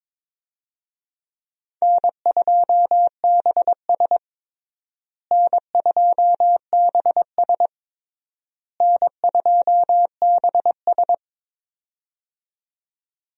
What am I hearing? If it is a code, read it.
Morse code, heard as a keyed tone, about 22 wpm, 700 Hz. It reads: N2BS N2BS N2BS
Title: USA Call Signs - 4 Characters - Repeated 3x - 22wpm